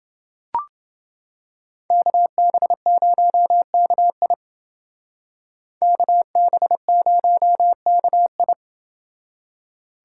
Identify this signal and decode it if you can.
Morse code, heard as a keyed tone, about 28 wpm, 700 Hz. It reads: KB0KI KB0KI